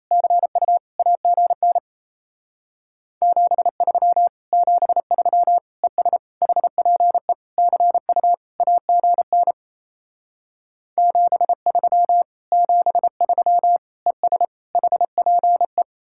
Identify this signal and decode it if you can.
Morse code, heard as a keyed tone, about 36 wpm, 700 Hz. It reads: CU AGN 73 73 ES HPE CU AGN 73 73 ES HPE